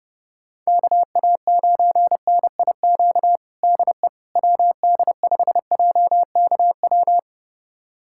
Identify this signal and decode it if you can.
Morse code, heard as a keyed tone, about 30 wpm, 700 Hz. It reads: KA9NIQ DE WD5JKW
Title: USA Callsign DE Callsign - 6 Characters - 30wpm